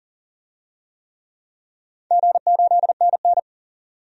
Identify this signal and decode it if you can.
Morse code, heard as a keyed tone, about 40 wpm, 700 Hz. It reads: G8NN
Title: DXCC-100 Call Signs - 4 Characters - 40wpm